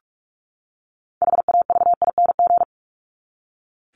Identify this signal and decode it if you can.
Morse code, heard as a keyed tone, about 44 wpm, 700 Hz. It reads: HAVING
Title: ING Suffix - Top 100 Words with Base Words 45wpm